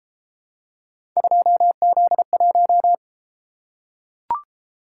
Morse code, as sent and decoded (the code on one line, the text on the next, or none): ..--- --.. .----
2Z1